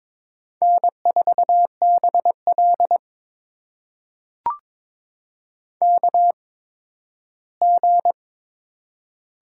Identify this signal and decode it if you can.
Morse code, heard as a keyed tone, about 22 wpm, spaced as for 13 wpm, 700 Hz. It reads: N4BL KG
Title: USA Call Signs - 8x Character Spacing - 22wpm